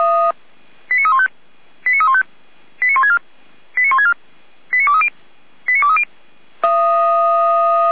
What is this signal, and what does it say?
Five-tone selective call (ZVEI), heard as quick runs of five tones, tones of about 80 ms. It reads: 89315 89315 89164 89164 89230 89230
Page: /455/SDR/ZVEI/89340/